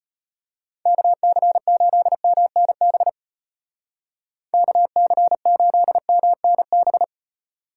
Morse code, heard as a keyed tone, about 32 wpm, 700 Hz. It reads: KC8MNB KC8MNB